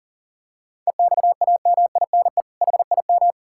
Morse code, heard as a keyed tone, about 40 wpm, 700 Hz. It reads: EXAMINE HIM
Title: Sentences from Top 1600 Words 40wpm